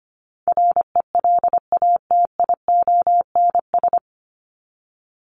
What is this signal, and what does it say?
Morse code, heard as a keyed tone, about 25 wpm, 700 Hz. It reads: RELATIONS
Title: ES:S Suffix - Top 100 Words with Base Words 25wpm